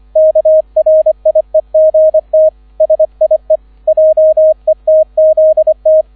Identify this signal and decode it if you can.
Morse code, heard as a keyed tone, about 25 wpm, 630 Hz, with radio noise. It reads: KRIEGT SIE JETZT
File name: 001 Erster Streich 25wpm